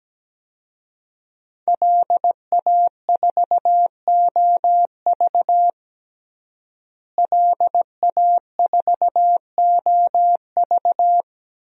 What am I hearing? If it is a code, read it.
Morse code, heard as a keyed tone, about 17 wpm, 700 Hz. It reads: LA4OV LA4OV
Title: DXCC-100 Call Signs - 5 Characters - Repeated 3x - 17wpm